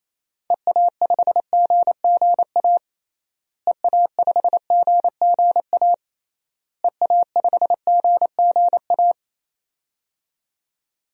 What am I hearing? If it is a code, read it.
Morse code, heard as a keyed tone, about 28 wpm, 700 Hz. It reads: EA5GGA EA5GGA EA5GGA
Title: DXCC-100 Call Signs - 6 Characters - Repeated 3x - 28wpm